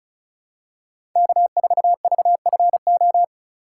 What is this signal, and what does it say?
Morse code, heard as a keyed tone, about 34 wpm, 700 Hz. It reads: K4VFO
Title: USA Call Signs - 4x Character Spacing - 35wpm